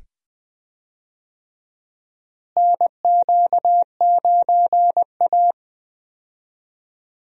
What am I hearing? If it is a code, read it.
Morse code, heard as a keyed tone, about 20 wpm, 700 Hz. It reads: NQ9A
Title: USA Call Signs - 4 Characters - Repeated 2x - 20wpm